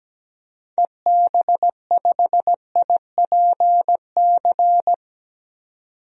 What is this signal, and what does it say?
Morse code, heard as a keyed tone, about 17 wpm, 700 Hz. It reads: EB5IPC
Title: DXCC-100 Call Signs - 6 Characters - 17wpm